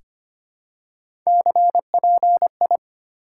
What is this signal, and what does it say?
Morse code, heard as a keyed tone, about 25 wpm, 700 Hz. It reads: CPI
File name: Top 90 QSO Elements - 4x Character Spacing - 25wpm